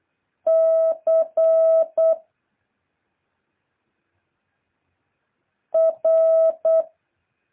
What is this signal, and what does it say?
Morse code, heard as a keyed tone, about 8 wpm, 650 Hz. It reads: C R
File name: jh8cbh / 2023cw